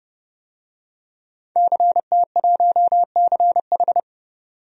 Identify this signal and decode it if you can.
Morse code, heard as a keyed tone, about 30 wpm, 700 Hz. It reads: CT1CH
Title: DXCC-100 Call Signs - 5 Characters - 30wpm